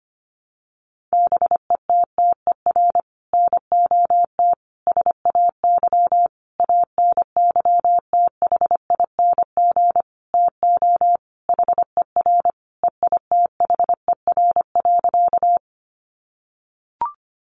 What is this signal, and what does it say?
Morse code, heard as a keyed tone, about 25 wpm, 700 Hz. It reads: BETTER NOT SAY ANYTHING TO HER EITHER.
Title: Sentences from Top 1000 Words 25wpm